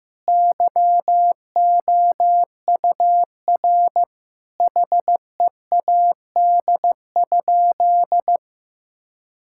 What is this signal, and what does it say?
Morse code, heard as a keyed tone, about 15 wpm, 700 Hz. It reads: YOUR HEAD?
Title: Sentences from Top 1000 Words - Review Part 5 - 15wpm